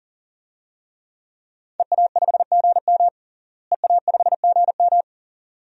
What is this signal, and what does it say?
Morse code, heard as a keyed tone, about 40 wpm, 700 Hz. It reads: EA5GM EA5GM
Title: DXCC-100 Call Signs - 5 Characters - Repeated 2x - 40wpm